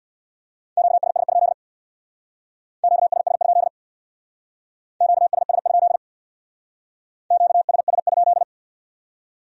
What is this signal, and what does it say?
Morse code, heard as a keyed tone, about 62 wpm, 700 Hz. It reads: QSS? QSS? QSS? QSS?
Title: Speed-Racing Q-Codes - 50wpm